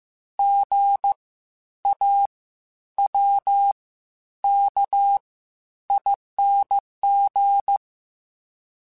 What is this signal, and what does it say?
Morse code, heard as a keyed tone, about 15 wpm, spaced as for 10 wpm, 800 Hz. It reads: GAWKING
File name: Spaced Alphas / Spaced Alphas mp3s / Gs sp @13